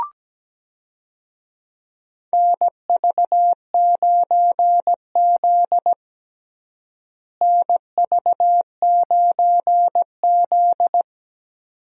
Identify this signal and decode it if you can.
Morse code, heard as a keyed tone, about 17 wpm, 700 Hz. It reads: NV9Z NV9Z
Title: USA Call Signs - 4 Characters - Repeated 2x - 17wpm